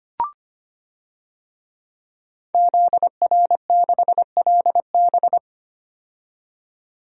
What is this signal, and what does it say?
Morse code, heard as a keyed tone, about 25 wpm, 700 Hz. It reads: ZR6LB
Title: DXCC-100 Call Signs - 5 Characters - 25wpm